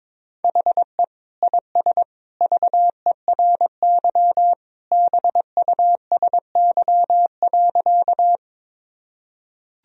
Morse code, heard as a keyed tone, about 22 wpm, 700 Hz. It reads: HE IS VERY BUSY.
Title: Sentences from Top 1000 Words - Review Part 3 - 22wpm